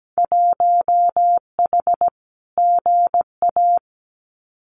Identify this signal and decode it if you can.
Morse code, heard as a keyed tone, about 17 wpm, 700 Hz. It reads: 1H GA
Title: Winter Field Day Exchanges 17wpm